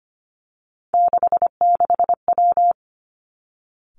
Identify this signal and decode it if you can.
Morse code, heard as a keyed tone, about 25 wpm, 700 Hz. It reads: 66W
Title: Three Challenging Letter-Number Combinations 25wpm